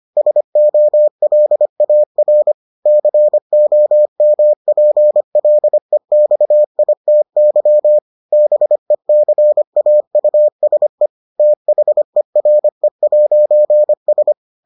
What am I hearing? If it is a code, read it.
Morse code, heard as a keyed tone, about 25 wpm, 600 Hz. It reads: SOLAR COMPLEXITY BECAUSE THERE'S